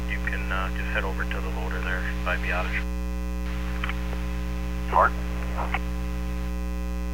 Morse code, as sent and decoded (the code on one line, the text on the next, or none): none